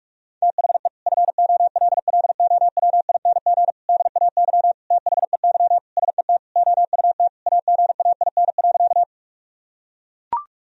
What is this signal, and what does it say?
Morse code, heard as a keyed tone, about 43 wpm, 700 Hz. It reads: THE FOLLOWING DAY THEY SET OUT AGAIN.
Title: Sentences from Top 1000 Words - Review Part 4 - 45wpm